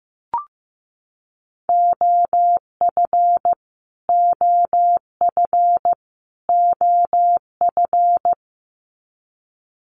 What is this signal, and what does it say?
Morse code, heard as a keyed tone, about 15 wpm, 700 Hz. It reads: OF OF OF
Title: Top 300 Words Repeated 15wpm